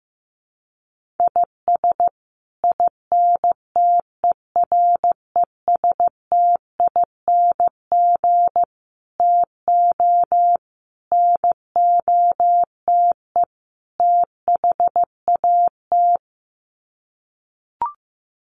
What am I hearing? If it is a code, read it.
Morse code, heard as a keyed tone, about 15 wpm, 700 Hz. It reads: IS INTERESTING TO NOTE THAT
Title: Sets of 5 Words Encore2 15wpm